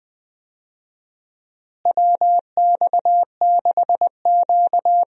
Morse code, heard as a keyed tone, about 20 wpm, 700 Hz. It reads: WX6Q
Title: USA Call Signs - 4 Characters - Repeated 2x - 20wpm